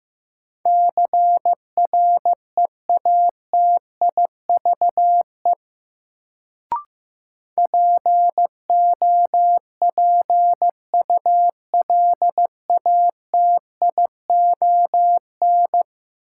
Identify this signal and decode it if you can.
Morse code, heard as a keyed tone, about 15 wpm, 700 Hz. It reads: CREATIVE POPULATION
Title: Sentences from Top 1300 Words - New Word Drill Alt - 15wpm